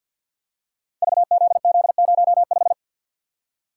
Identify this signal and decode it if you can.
Morse code, heard as a keyed tone, about 50 wpm, 700 Hz. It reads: VZZ05
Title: Code Groups - 5 Letter:Numbers - 50wpm